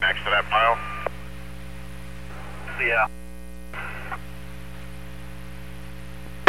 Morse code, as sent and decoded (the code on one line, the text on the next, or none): none